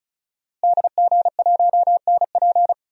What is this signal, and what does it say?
Morse code, heard as a keyed tone, about 34 wpm, 700 Hz. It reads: DG1NP
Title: DXCC-100 Call Signs - 5 Characters - Repeated 2x - 35wpm